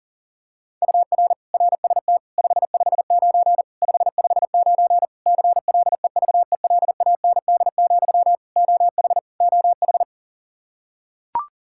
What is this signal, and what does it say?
Morse code, heard as a keyed tone, about 40 wpm, 700 Hz. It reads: UR RST 559 559 CLEVELAND, OH OH
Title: Practice QSO - Exchange 1 - Beginning - 40wpm